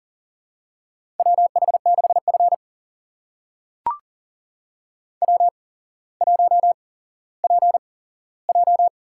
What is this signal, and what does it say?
Morse code, heard as a keyed tone, about 38 wpm, spaced as for 17 wpm, 700 Hz. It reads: WH6F W1PJ
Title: USA Call Signs - 4 Characters - 8x Character Spacing - 40wpm